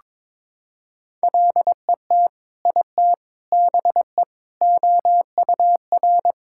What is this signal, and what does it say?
Morse code, heard as a keyed tone, about 22 wpm, 700 Hz. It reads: LET IT BE OUR
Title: Sentences from Top 1600 Words 22wpm